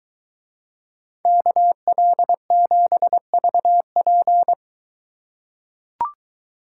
Morse code, heard as a keyed tone, about 22 wpm, 700 Hz. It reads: KL7VP